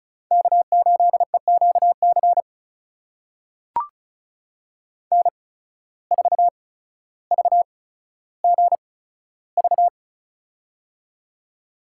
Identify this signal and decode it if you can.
Morse code, heard as a keyed tone, about 34 wpm, spaced as for 15 wpm, 700 Hz. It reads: K8EQC N4VGV
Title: USA Call Signs - 5 Characters - 8x Character Spacing - 35wpm